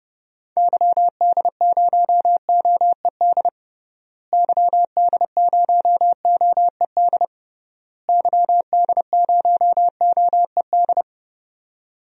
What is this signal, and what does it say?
Morse code, heard as a keyed tone, about 30 wpm, 700 Hz. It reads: YD0OED YD0OED YD0OED
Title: DXCC-100 Call Signs - 6 Characters - Repeated 3x - 30wpm